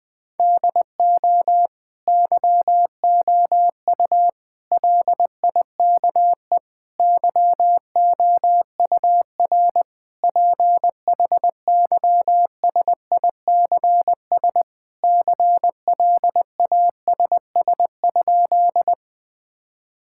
Morse code, heard as a keyed tone, about 20 wpm, 700 Hz. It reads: DO YOU LIKE YOUR PHYSICS CLASS?
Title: Sentences from Top 1000 Words - Review Part 5 - 20wpm